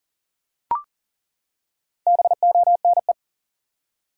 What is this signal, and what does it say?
Morse code, heard as a keyed tone, about 40 wpm, 700 Hz. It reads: BONE
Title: Four Letter Words - 40wpm